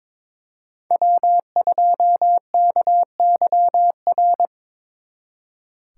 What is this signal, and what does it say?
Morse code, heard as a keyed tone, about 26 wpm, 700 Hz. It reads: W2KYR
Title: Speed Racing - USA 1x3 Call Signs 22wpm